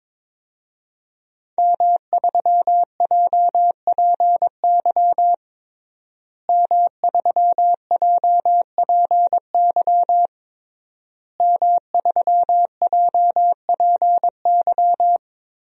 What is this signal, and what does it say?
Morse code, heard as a keyed tone, about 22 wpm, 700 Hz. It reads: M3JPY M3JPY M3JPY
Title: DXCC-100 Call Signs - 5 Characters - Repeated 3x - 22wpm